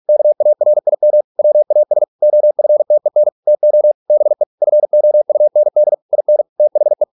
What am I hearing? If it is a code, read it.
Morse code, heard as a keyed tone, about 46 wpm, 600 Hz. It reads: KARIM WAS OFTEN TO BE FOUND IN THE